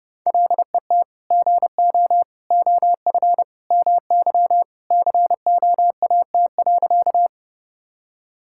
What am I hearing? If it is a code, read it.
Morse code, heard as a keyed tone, about 31 wpm, 700 Hz. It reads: LET GO OF MY COAT.
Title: Sentences from Top 1600 Words 30wpm